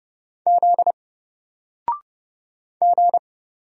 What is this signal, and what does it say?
Morse code, heard as a keyed tone, about 30 wpm, 700 Hz. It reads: Z G
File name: Single-Letter-Number-30wpm